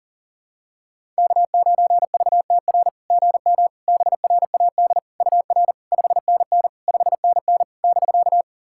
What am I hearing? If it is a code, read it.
Morse code, heard as a keyed tone, about 40 wpm, 690 Hz. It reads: K9VTR GM BRAD UR 5NN 5NN <BK>
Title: SOTA QSO Practice 40wpm